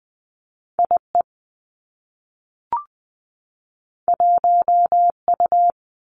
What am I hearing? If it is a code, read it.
Morse code, heard as a keyed tone, about 20 wpm, 700 Hz. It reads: I E 1 U
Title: Two Challenging Letter-Number Combinations 20wpm